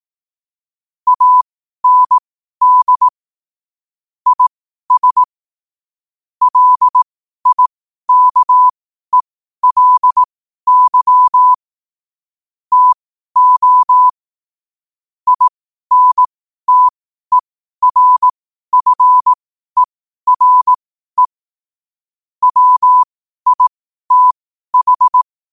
Morse code, heard as a keyed tone, about 18 wpm, spaced as for 12 wpm, 1000 Hz. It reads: AND IS LIKELY TO INTERFERE WITH
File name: qotdfiles / 12wpm